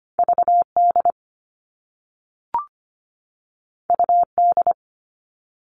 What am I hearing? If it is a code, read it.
Morse code, heard as a keyed tone, about 25 wpm, 700 Hz. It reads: VD UD
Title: All Cut Numbers - 2-Digit 25wpm